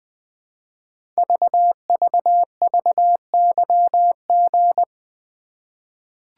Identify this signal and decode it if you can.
Morse code, heard as a keyed tone, about 20 wpm, 700 Hz. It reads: VVVYG
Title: Code Groups - 5 Letters - 20wpm